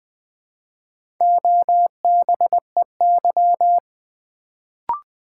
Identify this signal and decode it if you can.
Morse code, heard as a keyed tone, about 20 wpm, 700 Hz. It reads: OBEY
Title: Build A Word - 1 Letter At A Time - 3 to 5-Letter Words - 20wpm